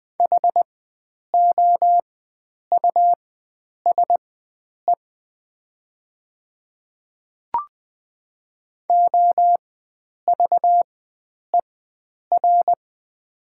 Morse code, heard as a keyed tone, about 20 wpm, spaced as for 9 wpm, 700 Hz. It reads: HOUSE OVER
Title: 4x Character Spacing - Top 300 Words - 20wpm